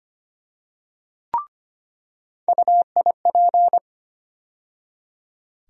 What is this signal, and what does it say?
Morse code, heard as a keyed tone, about 25 wpm, 700 Hz. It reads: UIP
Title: USA Call Sign Postfixes - 3 Letters 25wpm